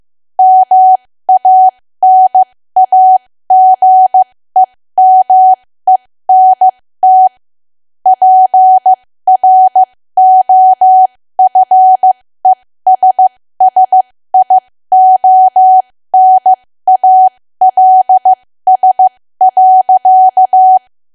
Morse code, heard as a keyed tone, about 15 wpm, spaced as for 13 wpm, 750 Hz. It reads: MANAGEMENT PROFESSIONALS.
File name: cwa / W1AW-Level2 / 13WPM